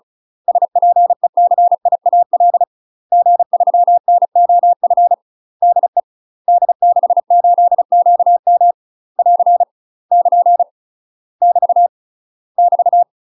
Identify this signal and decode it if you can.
Morse code, heard as a keyed tone, about 36 wpm, 700 Hz. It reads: SPECIAL G3NOF DE D68QM <AR> <KN> <BT> <BT>
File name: UK MASTER QSOs TRAINING / 35WPM / UK MASTER QSOs 0020